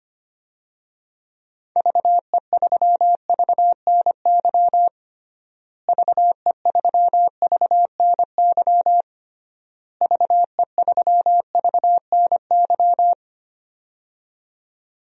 Morse code, heard as a keyed tone, about 25 wpm, 700 Hz. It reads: VE3VNY VE3VNY VE3VNY
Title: DXCC-100 Call Signs - 6 Characters - Repeated 3x - 25wpm